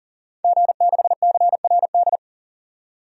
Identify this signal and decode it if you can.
Morse code, heard as a keyed tone, about 40 wpm, 700 Hz. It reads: G6CRD
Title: DXCC-100 Call Signs - 5 Characters - 40wpm